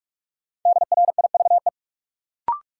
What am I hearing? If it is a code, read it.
Morse code, heard as a keyed tone, about 46 wpm, 700 Hz. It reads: DRIVE
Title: Sentences from Top 500 Words - New Word Drill - 45wpm